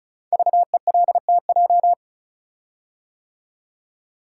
Morse code, heard as a keyed tone, about 36 wpm, 700 Hz. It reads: VELTJ
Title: Code Groups - 5 Letter:Numbers - 35wpm